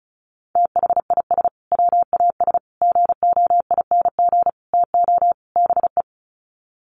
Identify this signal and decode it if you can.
Morse code, heard as a keyed tone, about 34 wpm, 700 Hz. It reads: THIS WAS GOING TO BE
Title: Sets of 5 Words Encore2 35wpm